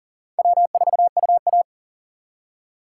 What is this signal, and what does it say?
Morse code, heard as a keyed tone, about 52 wpm, 700 Hz. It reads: W4UA